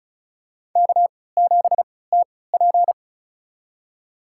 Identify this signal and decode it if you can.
Morse code, heard as a keyed tone, about 34 wpm, spaced as for 20 wpm, 700 Hz. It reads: K7TP
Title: USA Call Signs - 4 Characters - 3x Character Spacing - 35wpm